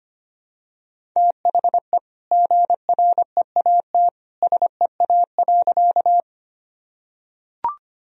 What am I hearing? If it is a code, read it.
Morse code, heard as a keyed tone, about 25 wpm, 700 Hz. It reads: THE GREAT SEA.
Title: Sentences from Top 1000 Words - Review Part 1 - 25wpm